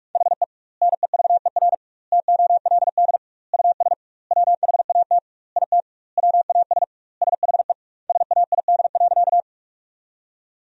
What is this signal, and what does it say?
Morse code, heard as a keyed tone, about 46 wpm, 700 Hz. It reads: HE NEVER TOLD US WHAT IT WAS SHE SAID.
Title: 2x Word-Spacing - Sentences from Top 400 Words - 45wpm